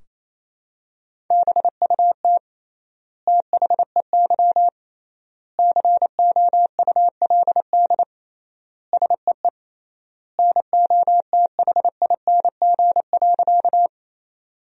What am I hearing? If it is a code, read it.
Morse code, heard as a keyed tone, about 28 wpm, 700 Hz. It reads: BUT THEY COULD SEE NOTHING.
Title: 3x Word-Spacing - Sentences from Top 500 Words - 28wpm